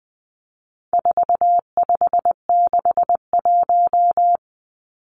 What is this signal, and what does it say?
Morse code, heard as a keyed tone, about 19 wpm, 700 Hz. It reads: 4561
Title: Numbers - 4-Digit 20wpm